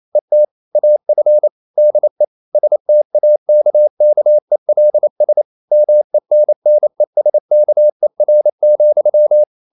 Morse code, heard as a keyed tone, about 28 wpm, 600 Hz. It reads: ET AF DE STAKKELS MENNESKER,
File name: Ferie QRQ 28wpm